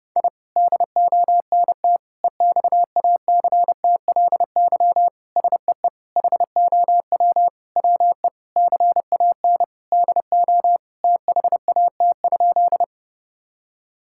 Morse code, heard as a keyed tone, about 30 wpm, 700 Hz. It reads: I DONT EXACTLY SEE HOW WE CAN DO THAT?
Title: Sentences from Top 1000 Words - Review Part 6 - 30wpm